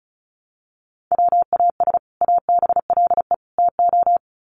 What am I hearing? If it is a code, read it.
Morse code, heard as a keyed tone, about 34 wpm, 700 Hz. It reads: WAS ABLE TO
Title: Sets of 3 Words 35wpm